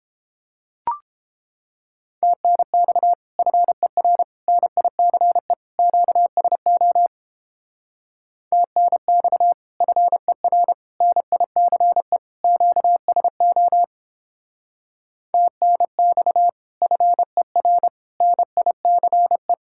Morse code, heard as a keyed tone, about 31 wpm, 700 Hz. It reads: TNX FER NICE QSO TNX FER NICE QSO TNX FER NICE